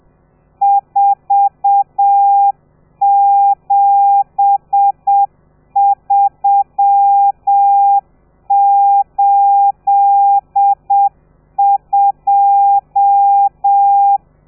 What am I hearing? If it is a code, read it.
Morse code, heard as a keyed tone, about 7 wpm, 800 Hz. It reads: 47382
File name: telegrafia / i2tfj / Lez13 30